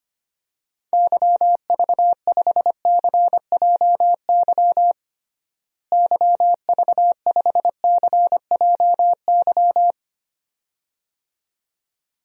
Morse code, heard as a keyed tone, about 25 wpm, 700 Hz. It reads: YV5CJY YV5CJY
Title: DXCC-100 Call Signs - 6 Characters - Repeated 2x - 25wpm